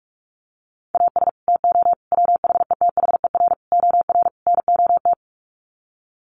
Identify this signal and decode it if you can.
Morse code, heard as a keyed tone, about 44 wpm, 700 Hz. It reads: AS TO WHETHER OR NOT
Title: Sets of 5 Words Encore 45wpm